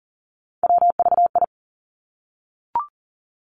Morse code, heard as a keyed tone, about 42 wpm, 700 Hz. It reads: WVI
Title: Three Challenging Letter-Number Combinations 40wpm